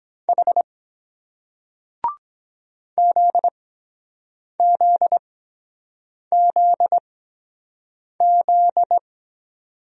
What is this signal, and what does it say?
Morse code, heard as a keyed tone, about 23 wpm, spaced as for 7 wpm, 700 Hz. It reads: H ZZZZ